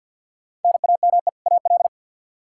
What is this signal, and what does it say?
Morse code, heard as a keyed tone, about 52 wpm, 700 Hz. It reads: NAME AL
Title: K1USN SST QSO Practice 50wpm